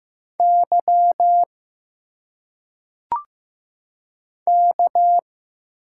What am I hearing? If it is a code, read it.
Morse code, heard as a keyed tone, about 15 wpm, 700 Hz. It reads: Y K
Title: Single Letters - Phonetic Pronunciation 15wpm